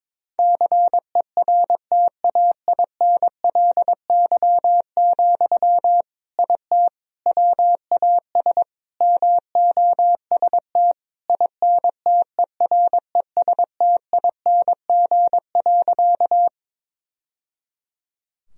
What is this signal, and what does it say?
Morse code, heard as a keyed tone, about 22 wpm, 700 Hz. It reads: CERTAINLY, IT WAS MOST INTERESTING.
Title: Sentences from Top 1000 Words - Review Part 1 - 22wpm